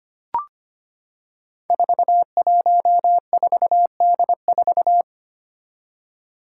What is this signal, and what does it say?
Morse code, heard as a keyed tone, about 25 wpm, 700 Hz. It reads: 414D4
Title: Five Challenging Letter-Number Combinations 25wpm